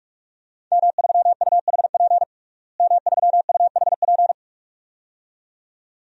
Morse code, heard as a keyed tone, about 43 wpm, 700 Hz. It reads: M3UHP M3UHP
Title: DXCC-100 Call Signs - 5 Characters - Repeated 2x - 45wpm